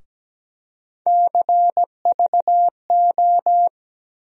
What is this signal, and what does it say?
Morse code, heard as a keyed tone, about 17 wpm, 700 Hz. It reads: CVO
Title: USA Call Sign Postfixes - 3 Letters 17wpm